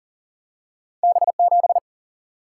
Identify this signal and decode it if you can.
Morse code, heard as a keyed tone, about 38 wpm, 700 Hz. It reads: B7
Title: Single-Letter Single-Number 40wpm